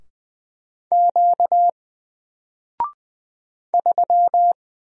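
Morse code, heard as a keyed tone, about 20 wpm, 700 Hz. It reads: Q 3
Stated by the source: Single letter and number Morse drill at 20 wpm, from N4LCD.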